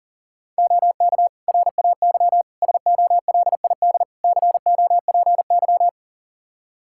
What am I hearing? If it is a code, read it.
Morse code, heard as a keyed tone, about 40 wpm, 700 Hz. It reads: OK RAY SOLID COPY
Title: Practice QSO - Exchange 3 - More Info - 40wpm